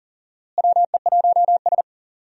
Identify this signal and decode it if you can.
Morse code, heard as a keyed tone, about 37 wpm, 700 Hz. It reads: WE1S
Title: USA Call Signs - 4 Characters - Repeated 2x - 40wpm